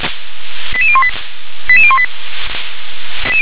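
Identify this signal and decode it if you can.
Five-tone selective call (ZVEI), heard as quick runs of five tones, tones of about 70 ms. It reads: 88018 88018
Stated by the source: Five-tone ZVEI selective call received on an SDR.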